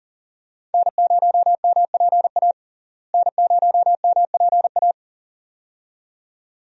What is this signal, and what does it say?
Morse code, heard as a keyed tone, about 40 wpm, 700 Hz. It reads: N0MPA N0MPA
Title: USA Call Signs - 5 Characters - Repeated 2x - 40wpm